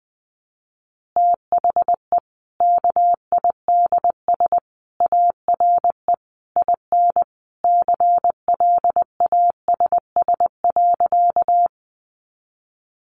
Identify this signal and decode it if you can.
Morse code, heard as a keyed tone, about 19 wpm, 700 Hz. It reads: THE KIDS ARE IN CLASS.
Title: Easy Phrases 20wpm